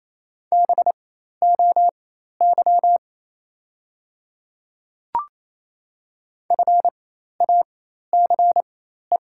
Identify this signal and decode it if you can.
Morse code, heard as a keyed tone, about 27 wpm, spaced as for 13 wpm, 690 Hz. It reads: BOY FACE
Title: 4x Character Spacing - Top 500 Words - 28wpm